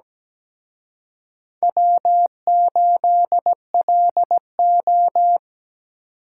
Morse code, heard as a keyed tone, about 20 wpm, 700 Hz. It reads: W8LO
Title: Speed Racing - USA Call Signs 17wpm